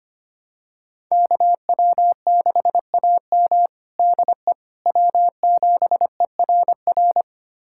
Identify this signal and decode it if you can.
Morse code, heard as a keyed tone, about 25 wpm, 700 Hz. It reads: KW6AM DE W7ERR
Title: USA Callsign DE Callsign - 5 Characters - 25wpm